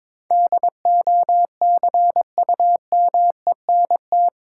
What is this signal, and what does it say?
Morse code, heard as a keyed tone, about 22 wpm, 700 Hz. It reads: DOCUMENT
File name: practice / 22wpm / Eight Letter Words - 22wpm